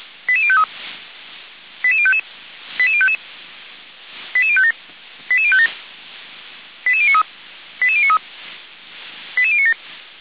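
Five-tone selective call (ZVEI), heard as quick runs of five tones, tones of about 70 ms. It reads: 88053 88055 88055 88057 88057 88093 88093 88097
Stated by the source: ZVEI five-tone selective call received by SDR, from fire brigade Feuerwehr Orlamünde audio server.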